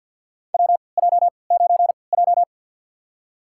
Morse code, heard as a keyed tone, about 52 wpm, spaced as for 29 wpm, 700 Hz. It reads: WJ9J
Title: USA Call Signs - 4 Characters - 3x Character Spacing - 50wpm